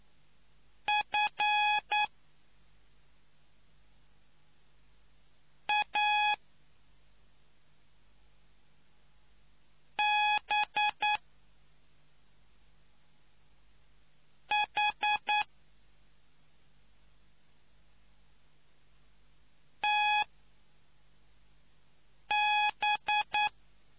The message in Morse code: F A B H TB

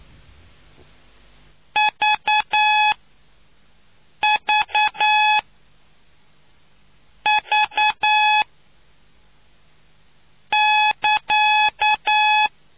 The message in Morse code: VVV<KA>

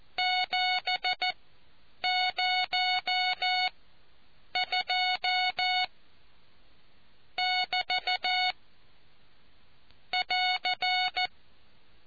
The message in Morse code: 702<BT> <AR>

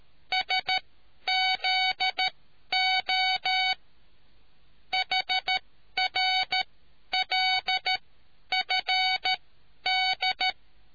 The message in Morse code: SZO HRLFD

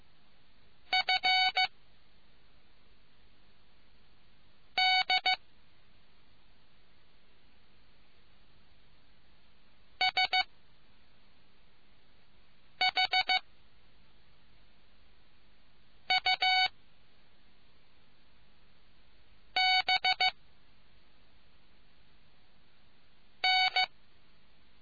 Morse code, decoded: F D S H U B N

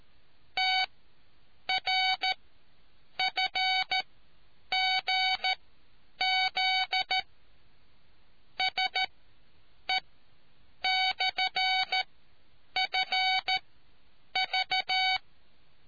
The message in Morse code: TRFGZ SE/FV

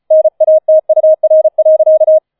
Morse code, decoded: NATUR.

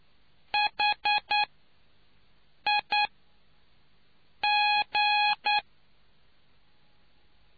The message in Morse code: HI G

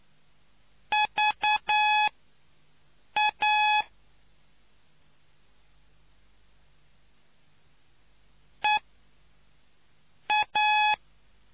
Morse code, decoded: VA EA

